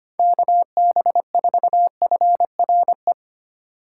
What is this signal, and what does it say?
Morse code, heard as a keyed tone, about 25 wpm, 700 Hz. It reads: KB4FRE
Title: USA Call Signs - 6 Characters - 4x Character Spacing - 25wpm